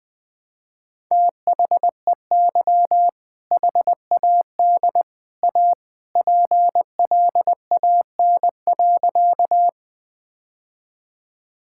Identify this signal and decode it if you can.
Morse code, heard as a keyed tone, about 19 wpm, 700 Hz. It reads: THEY HAD A PLAN.
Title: Sentences with 2-Word Phrases - 20wpm